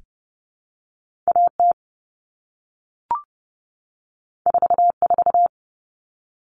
Morse code, heard as a keyed tone, about 30 wpm, 700 Hz. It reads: AT 44
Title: All Cut Numbers - 2-Digit 30wpm